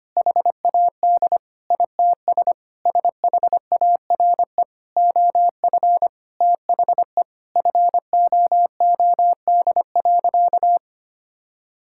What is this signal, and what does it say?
Morse code, heard as a keyed tone, about 25 wpm, 700 Hz. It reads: HAD ITS SHARE OF THE FOOD.
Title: Sentences from Top 1000 Words - Review Part 6 - 25wpm